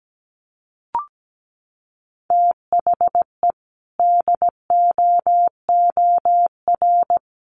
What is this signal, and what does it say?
Morse code, heard as a keyed tone, about 17 wpm, 700 Hz. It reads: THE DOOR
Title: Sentences from Top 1400 Words 17wpm